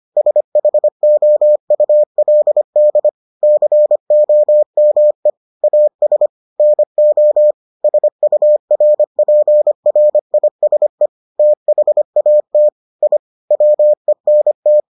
Morse code, heard as a keyed tone, about 25 wpm, 600 Hz. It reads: SHOULD COME AS NO SURPRISE THAT I WENT